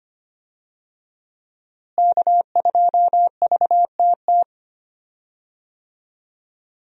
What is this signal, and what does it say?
Morse code, heard as a keyed tone, about 25 wpm, 700 Hz. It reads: K2VTT
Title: USA Call Signs - 5 Characters - Repeated 2x - 25wpm